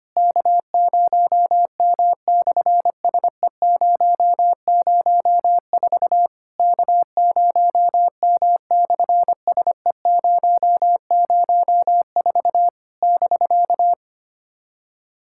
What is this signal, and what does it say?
Morse code, heard as a keyed tone, about 24 wpm, 700 Hz. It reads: K0M/SE004 K0M/SE004 <BK>
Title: SOTA QSO Practice 25wpm